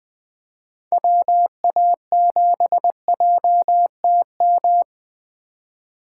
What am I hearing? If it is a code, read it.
Morse code, heard as a keyed tone, about 20 wpm, 700 Hz. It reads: WA7JTM
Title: 2023 SOTA Activator Call Signs - 20wpm